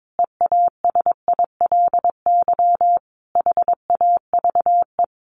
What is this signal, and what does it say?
Morse code, heard as a keyed tone, about 22 wpm, 700 Hz. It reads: EASILY HAVE
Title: Sentences from Top 1100 Words 22wpm